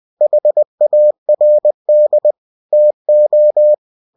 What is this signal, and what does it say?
Morse code, heard as a keyed tone, about 20 wpm, 600 Hz. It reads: HARD TO
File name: cw-books / winnie-the-pooh / winnie-20wpm-page-0062